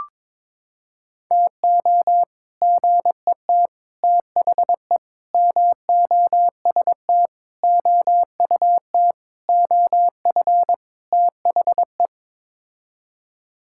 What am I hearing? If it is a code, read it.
Morse code, heard as a keyed tone, about 22 wpm, 700 Hz. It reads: TO GET THE MOST OUT OF THE
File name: practice / 22wpm / Sets of 7 Words - 22wpm